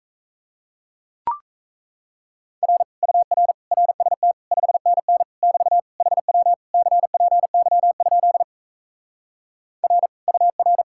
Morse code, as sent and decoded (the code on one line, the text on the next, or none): .-. / ..- .-. / .-. ... - / ..... -. -. / -...- / .... .-- / -.-. .--. -.-- ..--.. / .-. / ..- .-.
R UR RST 5NN <BT> HW CPY? R UR